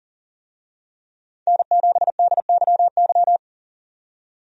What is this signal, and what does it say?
Morse code, heard as a keyed tone, about 40 wpm, 700 Hz. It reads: N7DYY
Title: USA Call Signs - 5 Characters - 40wpm